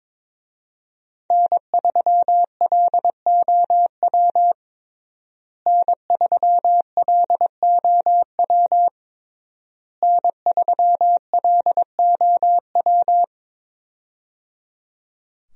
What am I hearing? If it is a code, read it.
Morse code, heard as a keyed tone, about 22 wpm, 700 Hz. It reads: N3LOW N3LOW N3LOW
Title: USA Call Signs - 5 Characters - Repeated 3x - 22wpm